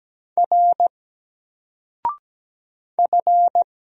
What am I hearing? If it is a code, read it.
Morse code, heard as a keyed tone, about 17 wpm, 700 Hz. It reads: R F
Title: Single Character (Letter:Number:Punctuation:Prosign) 17wpm